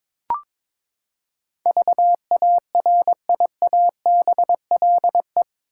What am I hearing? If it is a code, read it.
Morse code, heard as a keyed tone, about 22 wpm, 700 Hz. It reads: VARIABLE
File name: practice / 22wpm / Eight Letter Words - 22wpm